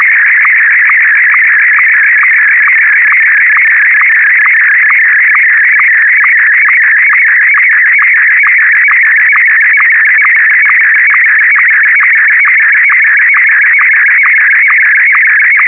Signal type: SSTV (slow-scan television)